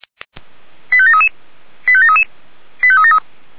Five-tone selective call (ZVEI), heard as quick runs of five tones, tones of about 70 ms. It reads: 85733 85733 85373